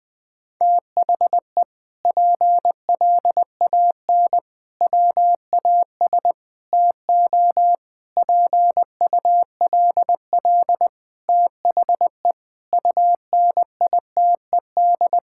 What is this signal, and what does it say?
Morse code, heard as a keyed tone, about 20 wpm, 700 Hz. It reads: THE PLAN WAS TO PULL THE UNITED
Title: Sentences with 8-Word Phrases - Spoken Beginning - 20wpm